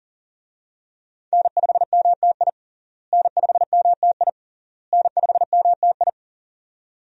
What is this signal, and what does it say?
Morse code, heard as a keyed tone, about 40 wpm, 700 Hz. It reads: N5MTI N5MTI N5MTI
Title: USA Call Signs - 5 Characters - Repeated 3x - 40wpm